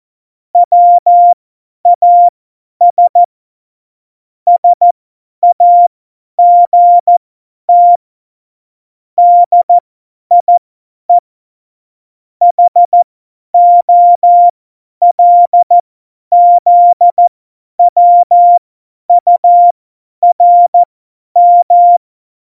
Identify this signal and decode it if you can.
Morse code, heard as a keyed tone, about 14 wpm, spaced as for 10 wpm, 700 Hz. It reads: WAS SAGT DIE HOLZWURM
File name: Morsekurs / Witze / Witz 19. Juni 2018, 50 F70